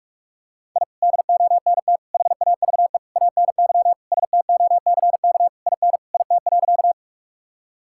Morse code, heard as a keyed tone, about 41 wpm, 700 Hz. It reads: I DONT HAVE ANY STOCK IN IT.
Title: Sentences from Top 1000 Words - Review Part 5 - 45wpm